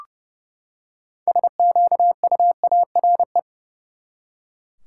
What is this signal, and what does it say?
Morse code, heard as a keyed tone, about 29 wpm, 700 Hz. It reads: SQUARE
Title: Six Letter Words - 30wpm